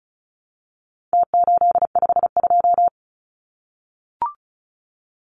Morse code, heard as a keyed tone, about 34 wpm, 700 Hz. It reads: T852
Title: Numbers - Contest Style - 4-Digit 35wpm